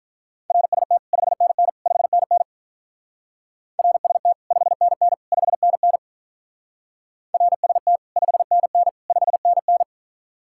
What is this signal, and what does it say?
Morse code, heard as a keyed tone, about 46 wpm, 700 Hz. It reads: RST 5NN 5NN RST 5NN 5NN RST 5NN 5NN